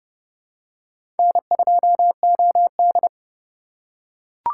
Morse code, heard as a keyed tone, about 29 wpm, 700 Hz. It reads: N2OD
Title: USA Call Signs - 4 Characters - 3x Character Spacing - 30wpm